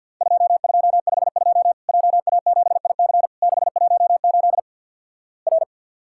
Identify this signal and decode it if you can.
Morse code, heard as a keyed tone, about 54 wpm, 700 Hz. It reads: 2252 JA7IC 618 R
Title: Worked All Europe DX Contest - QTC Exchange with Repeats - 50wpm